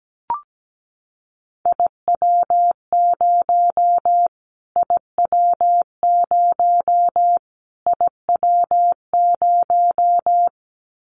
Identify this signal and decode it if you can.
Morse code, heard as a keyed tone, about 17 wpm, 700 Hz. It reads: IW0 IW0 IW0
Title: Callsign Prefixes for Top 25 DXCC - Repeated 3x - 17wpm